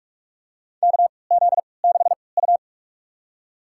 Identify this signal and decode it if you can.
Morse code, heard as a keyed tone, about 46 wpm, spaced as for 25 wpm, 700 Hz. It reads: KZ6U
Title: USA Call Signs - 4 Characters - 3x Character Spacing - 45wpm